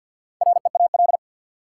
Morse code, heard as a keyed tone, about 53 wpm, 700 Hz. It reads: REAL
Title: Sentences from Top 300 Words - New Word Drill - 50wpm